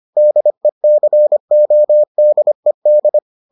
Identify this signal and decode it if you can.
Morse code, heard as a keyed tone, about 25 wpm, 600 Hz. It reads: DECODED